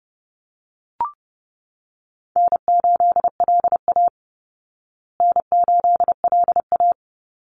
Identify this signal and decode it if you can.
Morse code, heard as a keyed tone, about 30 wpm, 700 Hz. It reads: N8LA N8LA
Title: Easy Call Signs Repeated 2x - 30wpm